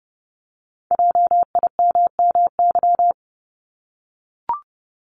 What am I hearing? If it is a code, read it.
Morse code, heard as a keyed tone, about 30 wpm, 700 Hz. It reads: JIMMY
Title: Multi-letter Phonemes Repeated 3x 30wpm